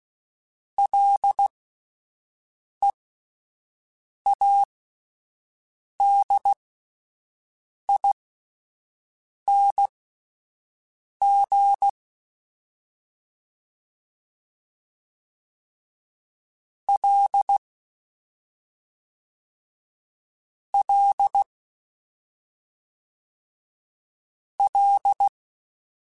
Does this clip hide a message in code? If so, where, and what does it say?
Yes: Morse code from 0.8 to 25.3 s, heard as a keyed tone, about 16 wpm, spaced as for 5 wpm, 800 Hz. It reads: LEADING L L L